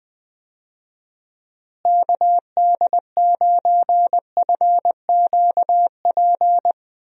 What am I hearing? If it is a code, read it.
Morse code, heard as a keyed tone, about 20 wpm, 700 Hz. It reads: KD9FQP
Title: USA Call Signs - 6 Characters - Repeated 2x - 20wpm